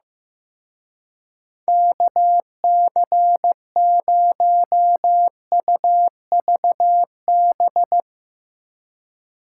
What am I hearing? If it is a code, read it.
Morse code, heard as a keyed tone, about 15 wpm, 700 Hz. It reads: KC0UVB
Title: Speed Racing - USA 2x3 Call Signs 15wpm